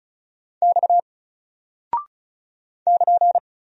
Morse code, heard as a keyed tone, about 36 wpm, 700 Hz. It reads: X <KN>
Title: Single Character (Letter:Number:Punctuation:Prosign) 35wpm